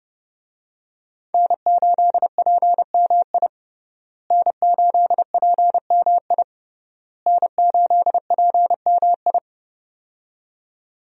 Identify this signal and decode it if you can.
Morse code, heard as a keyed tone, about 32 wpm, 700 Hz. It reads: N8PMI N8PMI N8PMI
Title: USA Call Signs - 5 Characters - Repeated 3x - 30wpm